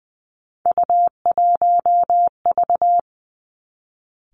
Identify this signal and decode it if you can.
Morse code, heard as a keyed tone, about 20 wpm, 700 Hz. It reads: U1V
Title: Three Challenging Letter-Number Combinations 20wpm